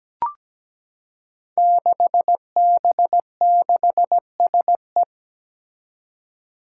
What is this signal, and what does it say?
Morse code, heard as a keyed tone, about 17 wpm, 700 Hz. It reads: 6B6SE
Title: Five Challenging Letter-Number Combinations 17wpm